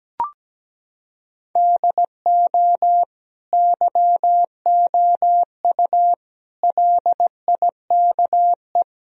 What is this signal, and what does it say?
Morse code, heard as a keyed tone, about 17 wpm, 700 Hz. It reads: DO YOU LIKE
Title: Sentences from Top 1000 Words - Review Part 3 - 17wpm